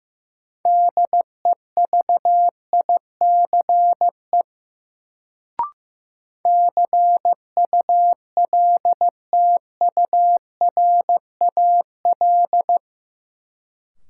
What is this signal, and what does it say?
Morse code, heard as a keyed tone, about 16 wpm, 700 Hz. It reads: DEVICE CULTURAL
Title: Sentences from Top 1500 Words - New Word Drill Alt - 15wpm